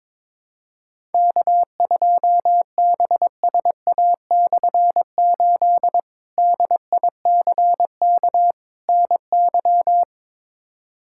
Morse code, heard as a keyed tone, about 22 wpm, 700 Hz. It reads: K2BSA/8 DICK NY
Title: K1USN SST QSO Practice - 22wpm